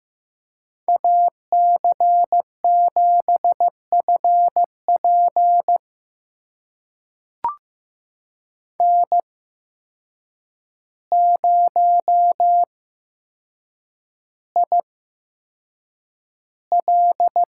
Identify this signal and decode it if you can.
Morse code, heard as a keyed tone, about 15 wpm, spaced as for 7 wpm, 700 Hz. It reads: AC7FP N0IL
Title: USA Call Signs - 5 Characters - 8x Character Spacing - 15wpm